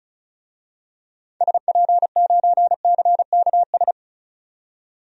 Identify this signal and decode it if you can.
Morse code, heard as a keyed tone, about 36 wpm, 700 Hz. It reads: SP9CKS
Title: DXCC-100 Call Signs - 6 Characters - 35wpm